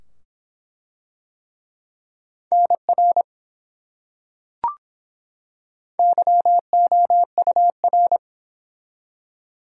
Morse code, heard as a keyed tone, about 24 wpm, 700 Hz. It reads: NR YOUR